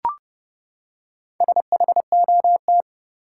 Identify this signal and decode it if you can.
Morse code, heard as a keyed tone, about 32 wpm, 700 Hz. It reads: SHOT
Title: Four Letter Words - 30wpm